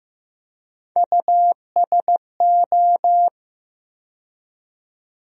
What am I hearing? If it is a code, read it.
Morse code, heard as a keyed tone, about 15 wpm, 700 Hz. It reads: USO
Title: USA Call Sign Postfixes - 3 Letters 15wpm